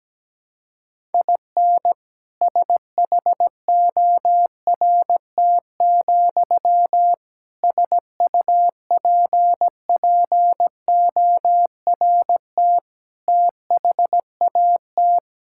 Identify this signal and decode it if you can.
Morse code, heard as a keyed tone, about 17 wpm, 700 Hz. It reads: IN SHORT, SUPPORT THAT
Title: Sentences from Top 1000 Words - Review Part 5 - 17wpm